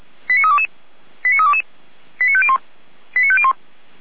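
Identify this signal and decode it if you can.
Five-tone selective call (ZVEI), heard as quick runs of five tones, tones of about 70 ms. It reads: 89322 89322 89501 89501
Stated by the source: ZVEI five-tone call 89340, received via SDR.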